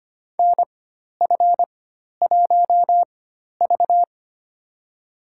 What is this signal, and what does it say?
Morse code, heard as a keyed tone, about 25 wpm, spaced as for 12 wpm, 700 Hz. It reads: NF1V
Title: USA Call Signs - 4 Characters - 4x Character Spacing - 25wpm